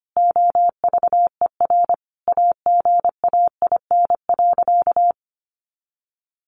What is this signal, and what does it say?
Morse code, heard as a keyed tone, about 25 wpm, 700 Hz. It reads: OVER AGAIN.
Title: Sentences from Top 500 Words 25wpm